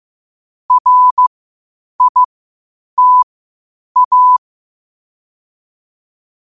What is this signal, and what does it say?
Morse code, heard as a keyed tone, about 15 wpm, spaced as for 8 wpm, 1000 Hz. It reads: RITA
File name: qotdfiles / 8wpm